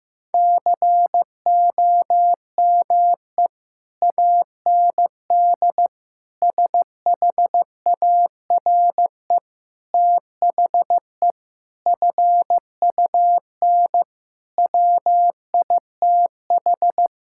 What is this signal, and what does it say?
Morse code, heard as a keyed tone, about 15 wpm, 700 Hz. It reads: COME AND SHARE THE FUN WITH
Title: Sentences from Top 1000 Words - Review Part 6 - 15wpm